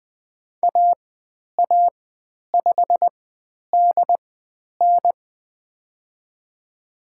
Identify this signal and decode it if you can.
Morse code, heard as a keyed tone, about 20 wpm, spaced as for 10 wpm, 700 Hz. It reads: AA5DN
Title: 095 Lesson 16g - Letter c - Call Signs 20wpm